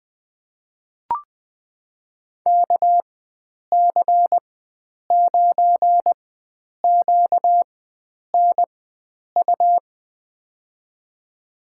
Morse code, heard as a keyed tone, about 20 wpm, spaced as for 9 wpm, 700 Hz. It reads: KC9QNU
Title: USA Call Signs - 4x Character Spacing - 20wpm